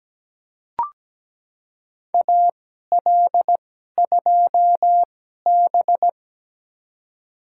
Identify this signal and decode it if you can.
Morse code, heard as a keyed tone, about 17 wpm, spaced as for 12 wpm, 700 Hz. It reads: AL2B
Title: USA Call Signs - 4 Characters - 2x Character Spacing - 17wpm